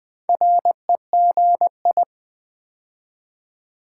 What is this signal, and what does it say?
Morse code, heard as a keyed tone, about 20 wpm, 700 Hz. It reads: REGI
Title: Build A Word - 2 Letters At A Time - 6-Letter Words - 20wpm